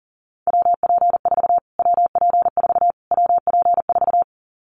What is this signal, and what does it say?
Morse code, heard as a keyed tone, about 40 wpm, 700 Hz. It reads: WP4 WP4 WP4
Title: Callsign Prefixes for Top 25 DXCC - Repeated 3x - 40wpm